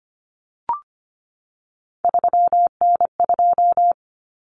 Morse code, heard as a keyed tone, about 25 wpm, 700 Hz. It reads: 3N2
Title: Common Cut Numbers - 3-Digit 25wpm